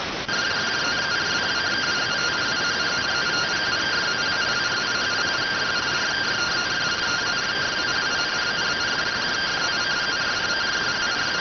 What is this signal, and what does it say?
Two-tone teleprinter signal, about 170 Hz shift.